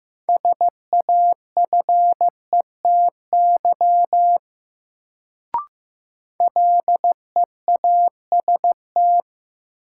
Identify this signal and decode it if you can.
Morse code, heard as a keyed tone, about 15 wpm, 700 Hz. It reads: SAFETY LEAST
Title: Sentences from Top 700 Words - New Word Drill - 15wpm